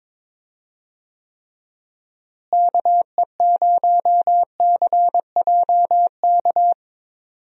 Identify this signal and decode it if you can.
Morse code, heard as a keyed tone, about 22 wpm, 700 Hz. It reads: KE0CJK
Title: USA Call Signs - 6 Characters - Repeated 2x - 22wpm